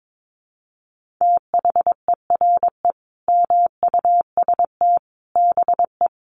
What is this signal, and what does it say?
Morse code, heard as a keyed tone, about 22 wpm, 700 Hz. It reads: THERE MUST BE
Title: Sentences from Top 1400 Words 22wpm